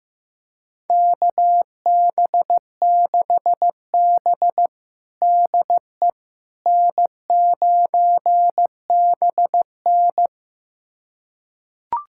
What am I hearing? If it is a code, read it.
Morse code, heard as a keyed tone, about 15 wpm, 700 Hz. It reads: KB6B DE N9BN
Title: USA Callsign DE Callsign - 4 Characters - 15wpm